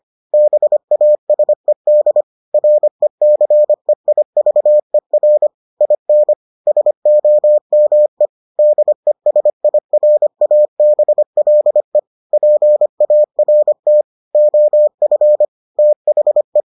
Morse code, heard as a keyed tone, about 25 wpm, 600 Hz. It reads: BASED RECEIVER IN SOME DESIRABLE PART OF THE